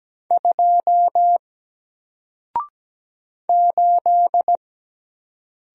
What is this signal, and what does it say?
Morse code, heard as a keyed tone, about 17 wpm, 700 Hz. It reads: 2 8
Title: Single Character (Letter:Number:Punctuation:Prosign) 17wpm